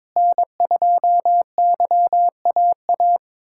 Morse code, heard as a keyed tone, about 21 wpm, 700 Hz. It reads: N2YAA
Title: USA Callsign DE Callsign 22wpm